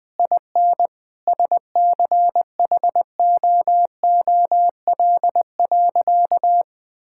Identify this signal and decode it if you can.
Morse code, heard as a keyed tone, about 20 wpm, 700 Hz. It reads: IN SCHOOL.
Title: Difficult Sentences with Hints - 20wpm